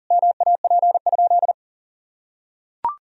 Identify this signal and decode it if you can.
Morse code, heard as a keyed tone, about 40 wpm, 700 Hz. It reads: MAP?
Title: Sentences from Top 1000 Words - Review Part 2 - 40wpm